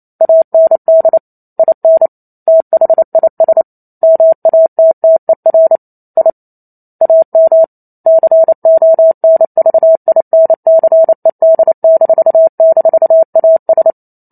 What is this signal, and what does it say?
Morse code, heard as a keyed tone, about 28 wpm, 650 Hz. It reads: AND IN THIS MATTER I AM CONVINCED--AS
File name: Chapter 2 - 28 WPM